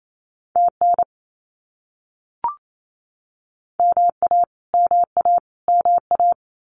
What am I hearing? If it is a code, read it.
Morse code, heard as a keyed tone, about 28 wpm, 700 Hz. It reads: TN MA MA MA
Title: US States Repeated 3x 28wpm